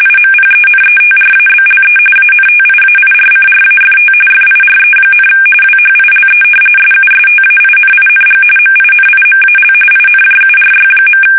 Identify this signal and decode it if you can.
RTTY, about 45.45 baud. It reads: IMPENETRABLE ANTIAIRCRAFT FIRE, WHILE SUSTAINING MINIMAL INJURY.
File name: fsk850 / battleshipnj